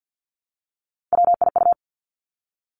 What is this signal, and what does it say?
Morse code, heard as a keyed tone, about 50 wpm, 700 Hz. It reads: WIU
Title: Three Challenging Letter-Number Combinations 50wpm